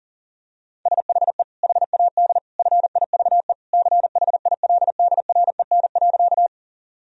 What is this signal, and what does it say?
Morse code, heard as a keyed tone, about 40 wpm, 700 Hz. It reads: SHE HAD FIVE CHILDREN.
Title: Sentences from Top 1000 Words - Review Part 2 - 40wpm